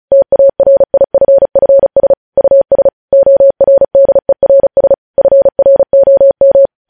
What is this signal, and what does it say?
Morse code, heard as a keyed tone, about 36 wpm, 550 Hz. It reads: TARIFFS US ORDERS FROM
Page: news headlines in morse code at 35 wpm